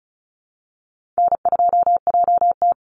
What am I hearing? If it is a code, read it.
Morse code, heard as a keyed tone, about 34 wpm, 700 Hz. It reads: N2JT
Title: Easy Call Signs Repeated 3x - 35wpm